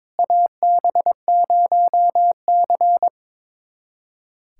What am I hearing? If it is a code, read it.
Morse code, heard as a keyed tone, about 22 wpm, 700 Hz. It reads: AB0C
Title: USA Callsign DE Callsign - 4 Characters - 22wpm